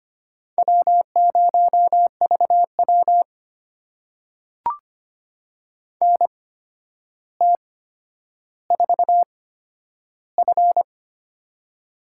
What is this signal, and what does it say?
Morse code, heard as a keyed tone, about 25 wpm, spaced as for 11 wpm, 700 Hz. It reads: W0VW NT4F
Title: USA Call Signs - 4 Characters - 8x Character Spacing - 25wpm